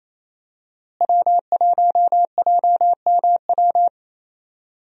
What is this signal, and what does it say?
Morse code, heard as a keyed tone, about 27 wpm, 700 Hz. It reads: W1JMW
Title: USA Call Signs - 5 Characters - Repeated 3x - 28wpm